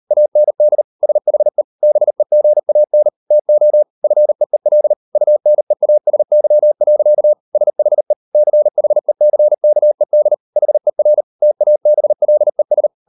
Morse code, heard as a keyed tone, about 38 wpm, 600 Hz. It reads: AND SHE BEGAN TO FEEL UNEASY. SHE CHECKED HER TABLES